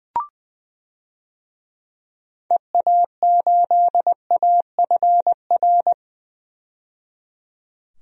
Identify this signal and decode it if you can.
Morse code, heard as a keyed tone, about 20 wpm, 700 Hz. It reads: EA8AFR
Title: DXCC-100 Call Signs - 6 Characters - 20wpm